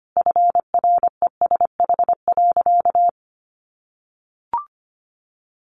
Morse code, heard as a keyed tone, about 25 wpm, 700 Hz. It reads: FRESH.
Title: Sentences from Top 1200 Words 25wpm